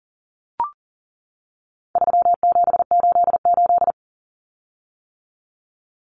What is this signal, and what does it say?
Morse code, heard as a keyed tone, about 40 wpm, 700 Hz. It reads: 3788
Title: Common Cut Numbers - 4-Digit 40wpm